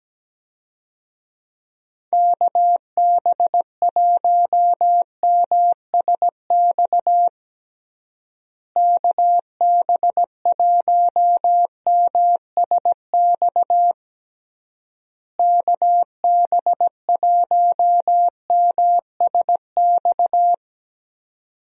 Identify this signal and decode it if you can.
Morse code, heard as a keyed tone, about 17 wpm, 700 Hz. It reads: KB1MSX KB1MSX KB1MSX
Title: USA Call Signs - 6 Characters - Repeated 3x - 17wpm